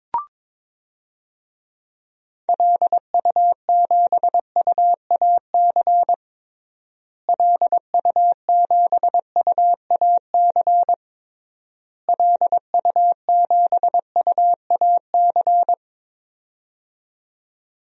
Morse code, heard as a keyed tone, about 22 wpm, 700 Hz. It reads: LU7UAC LU7UAC LU7UAC
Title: DXCC-100 Call Signs - 6 Characters - Repeated 3x - 22wpm